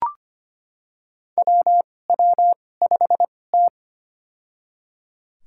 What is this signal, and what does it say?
Morse code, heard as a keyed tone, about 25 wpm, spaced as for 18 wpm, 700 Hz. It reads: WW5T
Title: USA Call Signs - 2x Character Spacing - 25wpm